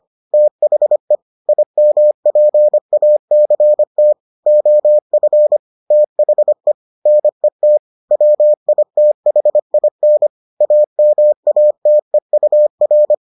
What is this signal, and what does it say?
Morse code, heard as a keyed tone, about 25 wpm, 600 Hz. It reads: THE IMPACT OF THE NET WITHIN AMATEUR